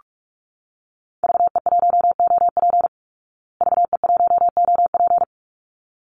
Morse code, heard as a keyed tone, about 44 wpm, 700 Hz. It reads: VE1OP VE1OP
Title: Call Signs Repeated 3x - 45wpm